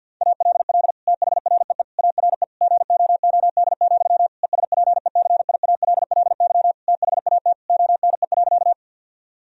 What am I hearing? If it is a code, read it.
Morse code, heard as a keyed tone, about 50 wpm, 700 Hz. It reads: ALL THREE ARE GOOD, ESPECIALLY THAT ONE.
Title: Sentences from Top 1000 Words - Review Part 4 - 50wpm